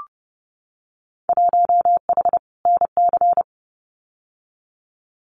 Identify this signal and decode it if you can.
Morse code, heard as a keyed tone, about 32 wpm, 700 Hz. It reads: 1H NC
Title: Winter Field Day Exchanges 30wpm